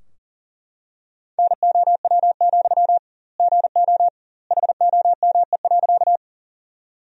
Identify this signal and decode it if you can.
Morse code, heard as a keyed tone, about 37 wpm, 700 Hz. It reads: NOW, GO HOME.
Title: 2x Word-Spacing - Sentences from Top 200 Words - 40wpm